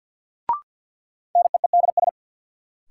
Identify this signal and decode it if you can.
Morse code, heard as a keyed tone, about 47 wpm, 700 Hz. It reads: NEEDS
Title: Sets of 3 Words - New Word Drill - 50wpm